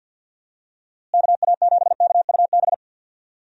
Morse code, heard as a keyed tone, about 46 wpm, 700 Hz. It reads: KA7KUB
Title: USA Call Signs - 6 Characters - Repeated 2x - 50wpm